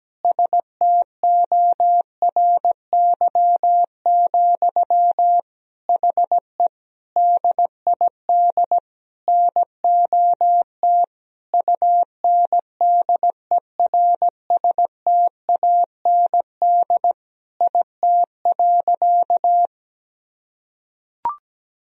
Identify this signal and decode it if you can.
Morse code, heard as a keyed tone, about 17 wpm, 700 Hz. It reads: STORY, HE DID NOT UNDERSTAND IT.
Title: Sentences from Top 1000 Words - Review Part 3 - 17wpm